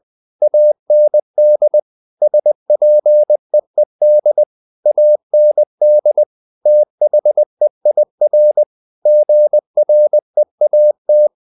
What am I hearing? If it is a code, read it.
Morse code, heard as a keyed tone, about 20 wpm, 600 Hz. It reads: AND SPEED AND THEIR GREAT
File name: the old man and the sea / the old man and the sea part 1 20wpm